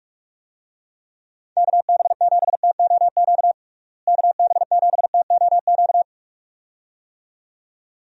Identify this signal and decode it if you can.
Morse code, heard as a keyed tone, about 46 wpm, 700 Hz. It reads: KB7TOQ KB7TOQ
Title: USA Call Signs - 6 Characters - Repeated 2x - 45wpm